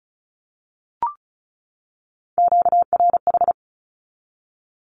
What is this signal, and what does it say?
Morse code, heard as a keyed tone, about 34 wpm, 700 Hz. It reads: QRH
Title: Q-Codes 35wpm